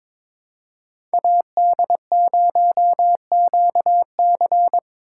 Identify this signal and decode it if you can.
Morse code, heard as a keyed tone, about 26 wpm, 700 Hz. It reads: AD0QC
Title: Speed Racing - USA 2x2 Call Signs 22wpm